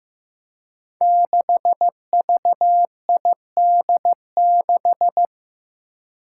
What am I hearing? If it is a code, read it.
Morse code, heard as a keyed tone, about 15 wpm, 700 Hz. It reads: 6VID6
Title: Five Challenging Letter-Number Combinations 15wpm